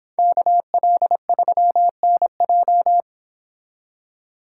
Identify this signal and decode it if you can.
Morse code, heard as a keyed tone, about 28 wpm, 700 Hz. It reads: KL3NJ